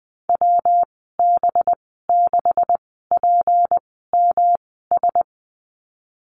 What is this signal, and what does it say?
Morse code, heard as a keyed tone, about 20 wpm, spaced as for 15 wpm, 700 Hz. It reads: WB6PMS
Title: 193 Lesson 31g - Letter v - Call Signs 20wpm